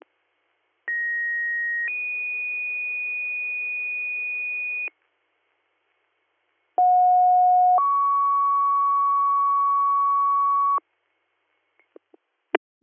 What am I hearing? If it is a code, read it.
Two-tone page, heard as a one-second tone, then a three-second tone, about 1900 then 2450 Hz, about 730 then 1120 Hz. It is heard from 0.9 to 10.8 s.